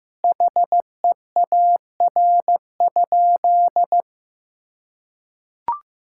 Morse code, heard as a keyed tone, about 15 wpm, 700 Hz. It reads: HEAR?
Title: Sentences from Top 1600 Words 15wpm